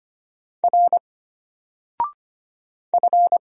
Morse code, heard as a keyed tone, about 25 wpm, 700 Hz. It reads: R F
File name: Single-Letter-Number-25wpm